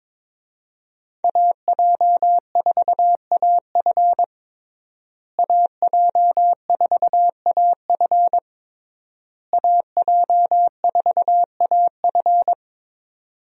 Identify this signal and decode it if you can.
Morse code, heard as a keyed tone, about 22 wpm, 700 Hz. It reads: AJ4AF AJ4AF AJ4AF
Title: USA Call Signs - 5 Characters - Repeated 3x - 22wpm